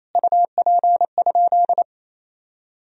Morse code, heard as a keyed tone, about 28 wpm, 700 Hz. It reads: UP?
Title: Sentences from Top 1000 Words - Review Part 1 - 28wpm